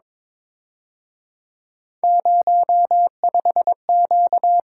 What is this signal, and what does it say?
Morse code, heard as a keyed tone, about 26 wpm, 700 Hz. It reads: 05Q
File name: Speed-Racing - Three Letter-Number Combinations - 22wpm